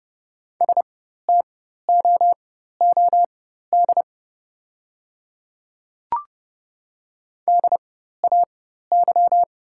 Morse code, heard as a keyed tone, about 30 wpm, spaced as for 14 wpm, 700 Hz. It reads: STOOD DAY
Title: 4x Character Spacing - Top 500 Words - 30wpm